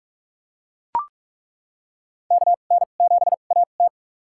Morse code, heard as a keyed tone, about 42 wpm, spaced as for 31 wpm, 690 Hz. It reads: KN7AT
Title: USA Call Signs - 5 Characters - 2x Character Spacing - 45wpm